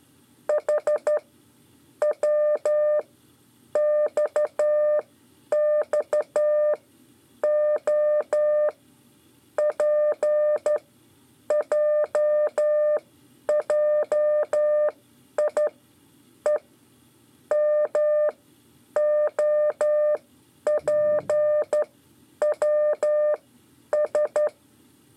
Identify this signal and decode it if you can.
Morse code, heard as a keyed tone, about 12 wpm, spaced as for 9 wpm, 600 Hz. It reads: H WXXO PJJIE MOPWS
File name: 5 LEZIONE RIPASSO 1